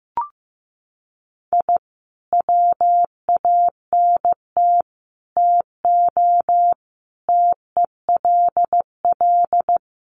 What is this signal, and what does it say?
Morse code, heard as a keyed tone, about 14 wpm, 700 Hz. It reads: I WANT TO TELL
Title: Sets of 5 Words Encore 15wpm